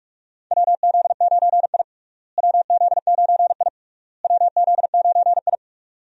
Morse code, heard as a keyed tone, about 41 wpm, 700 Hz. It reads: WZ9I WZ9I WZ9I
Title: USA Call Signs - 4 Characters - Repeated 3x - 45wpm